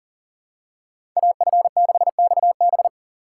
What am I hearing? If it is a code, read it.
Morse code, heard as a keyed tone, about 38 wpm, 690 Hz. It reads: AF6XB
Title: USA Call Signs - 5 Characters - 3x Character Spacing - 40wpm